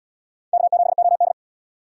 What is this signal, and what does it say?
Morse code, heard as a keyed tone, about 68 wpm, 700 Hz. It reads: K7OD